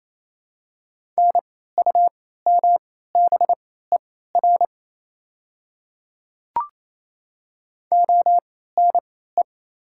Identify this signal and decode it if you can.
Morse code, heard as a keyed tone, about 28 wpm, spaced as for 16 wpm, 690 Hz. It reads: NUMBER ONE
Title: 3x Character Spacing - Top 100 Words - 28wpm